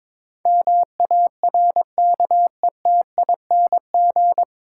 Morse code, heard as a keyed tone, about 22 wpm, 700 Hz. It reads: MARKETING
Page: Morse Code Ninja practice 22wpm